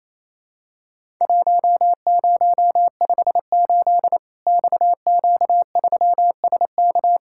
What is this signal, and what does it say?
Morse code, heard as a keyed tone, about 28 wpm, 700 Hz. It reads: 1058 XQ3SK
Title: Worked All Europe DX Contest - QTC Exchange - 28wpm